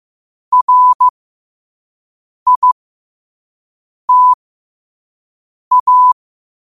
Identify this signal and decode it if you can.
Morse code, heard as a keyed tone, about 15 wpm, spaced as for 5 wpm, 1000 Hz. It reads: RITA
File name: qotdfiles / 5wpm